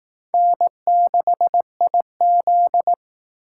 Morse code, heard as a keyed tone, about 18 wpm, 700 Hz. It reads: N6IZ
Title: Speed Racing - USA 1x2 Call Signs 15wpm